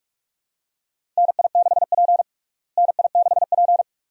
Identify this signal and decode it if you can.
Morse code, heard as a keyed tone, about 46 wpm, 700 Hz. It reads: NI6P NI6P
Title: USA Call Signs - 4 Characters - Repeated 2x - 45wpm